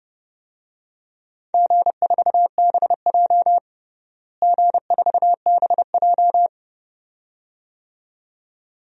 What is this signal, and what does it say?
Morse code, heard as a keyed tone, about 30 wpm, 700 Hz. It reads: G4BJ G4BJ
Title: DXCC-100 Call Signs - 4 Characters - Repeated 2x - 30wpm